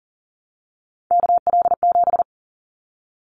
Morse code, heard as a keyed tone, about 40 wpm, 700 Hz. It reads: KL7
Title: Callsign Prefixes for Top 100 DXCC - 40wpm